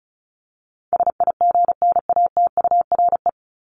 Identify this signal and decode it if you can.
Morse code, heard as a keyed tone, about 34 wpm, 700 Hz. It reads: SIGNATURE
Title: Top 1751-2000 Words 35wpm